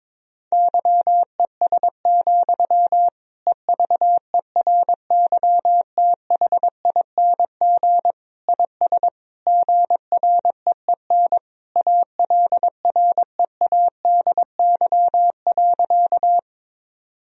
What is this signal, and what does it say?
Morse code, heard as a keyed tone, about 22 wpm, 700 Hz. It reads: YES, EVERYTHING IS GREEN ALREADY.
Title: Sentences from Top 1000 Words - Review Part 2 - 22wpm